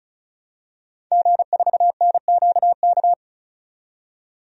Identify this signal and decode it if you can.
Morse code, heard as a keyed tone, about 36 wpm, 700 Hz. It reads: G4NQK
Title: DXCC-100 Call Signs - 5 Characters - 35wpm